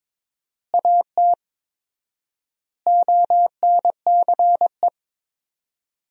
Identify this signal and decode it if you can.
Morse code, heard as a keyed tone, about 22 wpm, 700 Hz. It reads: AT ONCE
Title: 4x Word-Spacing - Sentences from Top 300 Words - 22wpm